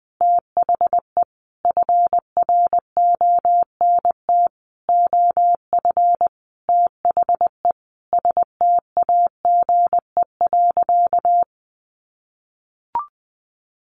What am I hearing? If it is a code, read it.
Morse code, heard as a keyed tone, about 20 wpm, 700 Hz. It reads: THE FRONT OF THE STAGE.
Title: Sentences from Top 1400 Words 20wpm